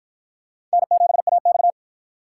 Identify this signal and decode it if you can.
Morse code, heard as a keyed tone, about 53 wpm, 700 Hz. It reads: N7AX